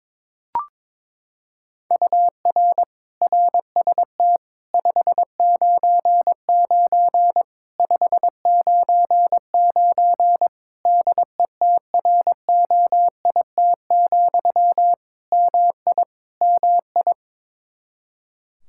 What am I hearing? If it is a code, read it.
Morse code, heard as a keyed tone, about 21 wpm, 700 Hz. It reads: UR RST 599 599 DETROIT, MI MI
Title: Practice QSO - Exchange 1 - Beginning - 22wpm